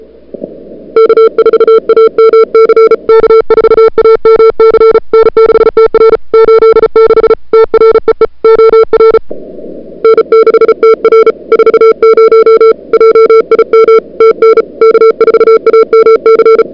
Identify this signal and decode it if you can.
Morse code, heard as a keyed tone, about 35 wpm, 440 Hz. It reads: K4AMC K4AMC N6TR 86 TREE OR N6TR 40 JIM TN K4AMC
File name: n6tr / Fab2007CWSprintHour1